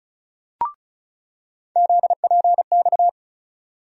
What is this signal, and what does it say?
Morse code, heard as a keyed tone, about 36 wpm, 700 Hz. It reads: ZPX
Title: USA Call Sign Postfixes - 3 Letters 35wpm